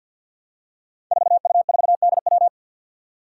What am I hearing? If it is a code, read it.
Morse code, heard as a keyed tone, about 47 wpm, 700 Hz. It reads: 4U4DW
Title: Five Challenging Letter-Number Combinations 50wpm